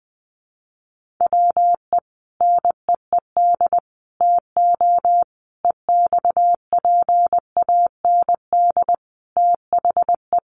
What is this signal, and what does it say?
Morse code, heard as a keyed tone, about 20 wpm, 700 Hz. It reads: WE NEED TO EXPAND THE
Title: Sentences from Top 1400 Words 20wpm